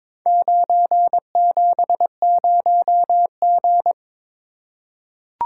Morse code, heard as a keyed tone, about 22 wpm, 700 Hz. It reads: 970G
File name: practice / 22wpm / Four Letter-Number Combinations - 22wpm